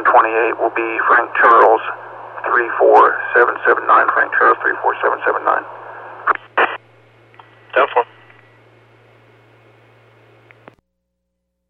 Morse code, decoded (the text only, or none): none